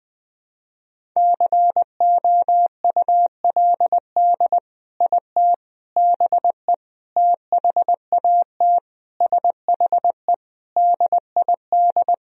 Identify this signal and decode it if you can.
Morse code, heard as a keyed tone, about 20 wpm, 700 Hz. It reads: COULD IT BE THAT SHE DID
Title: Sentences from Top 1000 Words - Review Part 1 - 20wpm